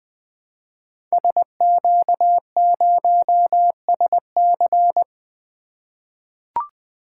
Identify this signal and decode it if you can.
Morse code, heard as a keyed tone, about 20 wpm, 700 Hz. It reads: SQ0SC
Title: Code Groups - 5 Letter:Numbers - 20wpm